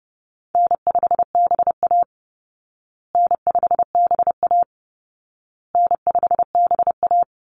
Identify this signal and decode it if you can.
Morse code, heard as a keyed tone, about 30 wpm, 700 Hz. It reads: N5BA N5BA N5BA
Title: Easy Call Signs Repeated 3x - 30wpm